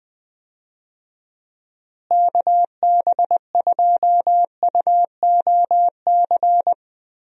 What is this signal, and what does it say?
Morse code, heard as a keyed tone, about 20 wpm, 700 Hz. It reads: KB2UOC
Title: USA Call Signs - 6 Characters - Repeated 2x - 20wpm